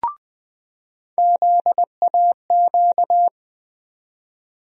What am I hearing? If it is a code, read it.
Morse code, heard as a keyed tone, about 20 wpm, 700 Hz. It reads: ZAQ
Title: USA Call Sign Postfixes - 3 Letters 20wpm